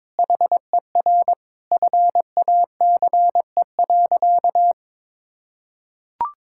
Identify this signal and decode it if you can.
Morse code, heard as a keyed tone, about 22 wpm, 700 Hz. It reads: HER FACE.
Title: Sentences from Top 1600 Words 22wpm